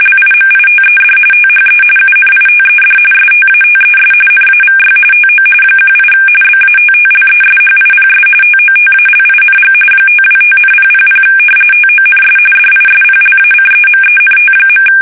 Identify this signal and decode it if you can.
RTTY, about 45.45 baud. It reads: LIMITED THE UNITED STATES NAVY TO 18 BATTLESHIPS TOTALING 525,000 TONS DISPLACEMENT.
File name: fsk850 / battleshipnj